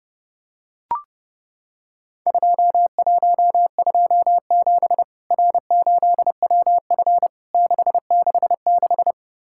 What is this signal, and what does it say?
Morse code, heard as a keyed tone, about 32 wpm, 700 Hz. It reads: 2127 R8WF 666
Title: Worked All Europe DX Contest - QTC Exchange - 30wpm